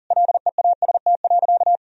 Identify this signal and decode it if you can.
Morse code, heard as a keyed tone, about 40 wpm, 700 Hz. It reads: LEAST.
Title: Sentences from Top 1000 Words - Review Part 2 - 40wpm